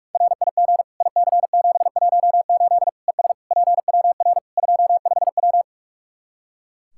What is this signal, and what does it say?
Morse code, heard as a keyed tone, about 46 wpm, 700 Hz. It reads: RIG IC718 ES PWR 25W
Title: Practice QSO - Exchange 2 - Info - 45wpm